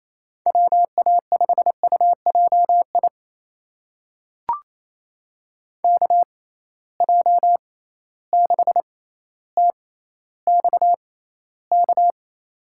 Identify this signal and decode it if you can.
Morse code, heard as a keyed tone, about 28 wpm, spaced as for 14 wpm, 700 Hz. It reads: WA5UJI KJ6TXK
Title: USA Call Signs - 6 Characters - 6x Character Spacing - 28wpm